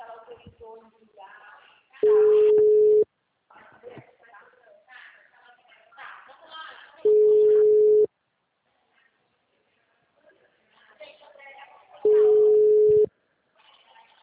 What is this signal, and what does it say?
Ringback tone (Europe, 425 Hz), with speech.
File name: out- -2049- - - .61993